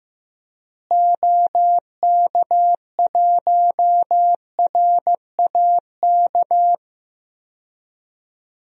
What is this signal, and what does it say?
Morse code, heard as a keyed tone, about 15 wpm, 700 Hz. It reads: OK1RAK
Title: DXCC-100 Call Signs - 6 Characters - Repeated 3x - 15wpm